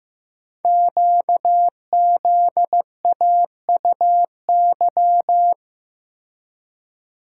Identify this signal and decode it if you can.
Morse code, heard as a keyed tone, about 15 wpm, 700 Hz. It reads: QZAUY
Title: Code Groups - 5 Letters - 15wpm